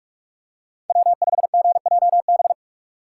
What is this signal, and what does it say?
Morse code, heard as a keyed tone, about 44 wpm, 700 Hz. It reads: W5GJB